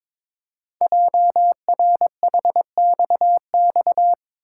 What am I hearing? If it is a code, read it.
Morse code, heard as a keyed tone, about 22 wpm, 700 Hz. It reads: JRHXX
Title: Code Groups - 5 Letters - 22wpm